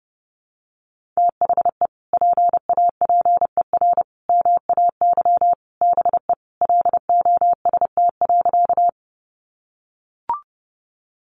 Morse code, heard as a keyed tone, about 30 wpm, 700 Hz. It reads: THE PAPER MAY BE LOST.
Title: Sentences from Top 1400 Words 30wpm